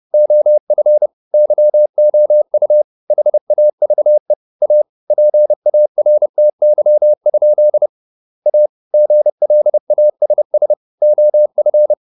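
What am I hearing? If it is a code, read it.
Morse code, heard as a keyed tone, about 30 wpm, 600 Hz. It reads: OF YOU HAVE A PARTY? A GLASS OF
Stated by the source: Morse code at 30 wpm, the novel Never by Ken Follett in English, from CW med Gnister.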